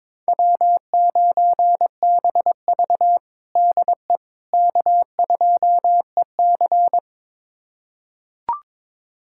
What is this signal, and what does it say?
Morse code, heard as a keyed tone, about 22 wpm, 700 Hz. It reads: W9BV DE K2EC
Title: USA Callsign DE Callsign - 4 Characters - 22wpm